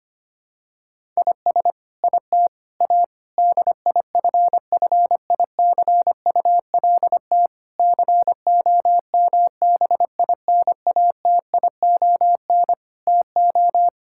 Morse code, heard as a keyed tone, about 25 wpm, 700 Hz. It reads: IS IT A DIFFICULT COMBINATION TO
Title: Sentences from Top 1000 Words - Review Part 6 - 25wpm